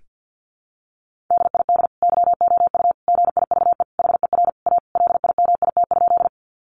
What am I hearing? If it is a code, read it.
Morse code, heard as a keyed tone, about 48 wpm, 700 Hz. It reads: DID YOU GIVE HER A LIMIT?
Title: Sentences from Top 1500 Words 50wpm